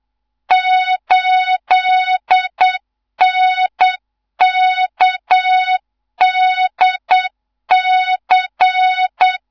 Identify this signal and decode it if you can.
Morse code, heard as a keyed tone, about 8 wpm, 3700 Hz, with noise. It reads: 8NKDC
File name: Morze-HA2MN / hang / 15 GYAKORLAT